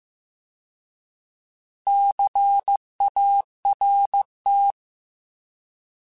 Morse code, heard as a keyed tone, about 15 wpm, 800 Hz. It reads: CART